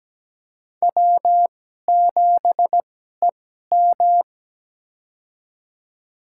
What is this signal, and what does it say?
Morse code, heard as a keyed tone, about 17 wpm, spaced as for 12 wpm, 700 Hz. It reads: W7EM
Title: USA Call Signs - 2x Character Spacing - 17wpm